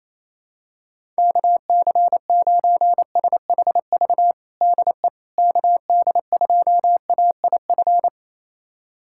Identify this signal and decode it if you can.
Morse code, heard as a keyed tone, about 28 wpm, 700 Hz. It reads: KC9SHV DE KD2AIF
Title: USA Callsign DE Callsign - 6 Characters - 28wpm